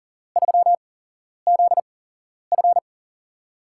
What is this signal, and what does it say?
Morse code, heard as a keyed tone, about 40 wpm, spaced as for 11 wpm, 700 Hz. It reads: 3ZF
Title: USA Call Signs - 5 Characters - 8x Character Spacing - 40wpm